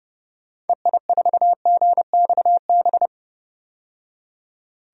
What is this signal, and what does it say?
Morse code, heard as a keyed tone, about 30 wpm, 700 Hz. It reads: EI4GXB
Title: DXCC-100 Call Signs - 6 Characters - 30wpm